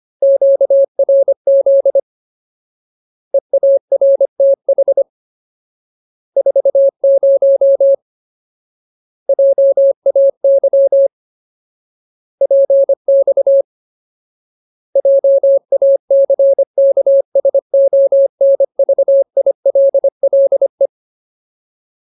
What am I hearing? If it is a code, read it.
Morse code, heard as a keyed tone, about 25 wpm, 550 Hz. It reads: QRZ EARTH 40 JAY PX JACKSONVILLE
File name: Ultimate Word List / 25WPM Ultimate